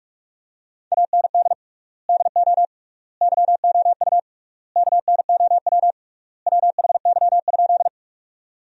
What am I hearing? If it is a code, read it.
Morse code, heard as a keyed tone, about 46 wpm, 700 Hz. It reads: AND DO YOU KNOW WHY?
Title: 3x Word-Spacing - Sentences from Top 200 Words - 45wpm